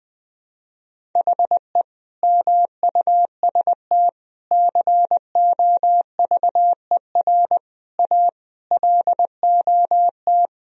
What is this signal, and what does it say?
Morse code, heard as a keyed tone, about 20 wpm, 700 Hz. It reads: HE MUST COVER A LOT
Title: Sentences from Top 500 Words and Abbreviations 20wpm